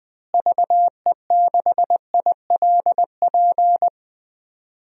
Morse code, heard as a keyed tone, about 21 wpm, 700 Hz. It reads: VE6ILP
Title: DXCC-100 Call Signs - 6 Characters - Repeated 2x - 20wpm